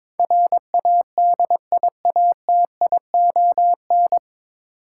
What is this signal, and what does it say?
Morse code, heard as a keyed tone, about 22 wpm, 700 Hz. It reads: RADIATION
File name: 9 Letter Words - First 3 Letters Given - 22wpm